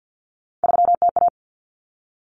Morse code, heard as a keyed tone, about 50 wpm, 700 Hz. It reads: 3TA
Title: Common Cut Numbers - 3-Digit 50wpm